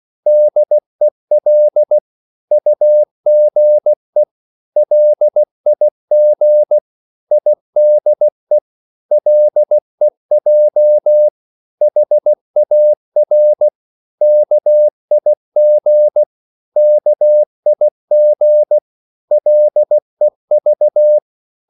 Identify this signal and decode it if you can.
Morse code, heard as a keyed tone, about 16 wpm, 600 Hz. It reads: DEL UGE LIG IDE LEJ HAR KIG KIG LEV
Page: Korte danske ord 16wpm